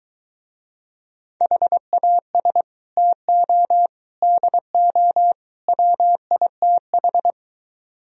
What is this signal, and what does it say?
Morse code, heard as a keyed tone, about 23 wpm, 700 Hz. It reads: HAS TO DO WITH